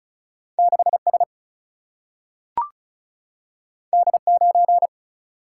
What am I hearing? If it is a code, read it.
Morse code, heard as a keyed tone, about 36 wpm, 700 Hz. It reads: 6 S D 9
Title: Single-Letter Single-Number 35wpm